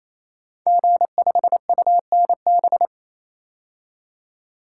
Morse code, heard as a keyed tone, about 28 wpm, 700 Hz. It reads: G5UNB
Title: Code Groups - 5 Letter:Numbers - 28wpm